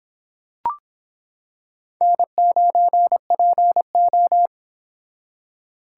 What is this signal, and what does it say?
Morse code, heard as a keyed tone, about 26 wpm, 700 Hz. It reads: N9PO